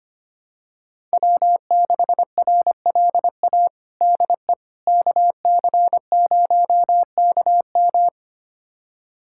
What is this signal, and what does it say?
Morse code, heard as a keyed tone, about 25 wpm, 700 Hz. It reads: W6RLA DE KC0KM
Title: USA Callsign DE Callsign - 5 Characters - 25wpm